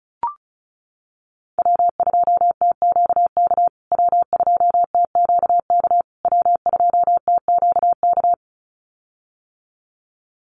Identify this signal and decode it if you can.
Morse code, heard as a keyed tone, about 34 wpm, 700 Hz. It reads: W2TQK W2TQK W2TQK
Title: USA Call Signs Repeated 3x 35wpm